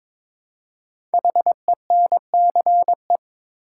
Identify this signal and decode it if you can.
Morse code, heard as a keyed tone, about 22 wpm, 700 Hz. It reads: HENCE
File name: 5 Letter Words - First 3 Letters Given - 22wpm